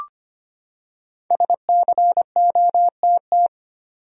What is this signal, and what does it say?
Morse code, heard as a keyed tone, about 25 wpm, 700 Hz. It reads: SCOTT
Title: USA Common Names - 25wpm